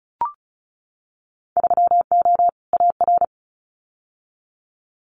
Morse code, heard as a keyed tone, about 36 wpm, 700 Hz. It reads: 3O AR
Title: Winter Field Day Exchanges Encore 35wpm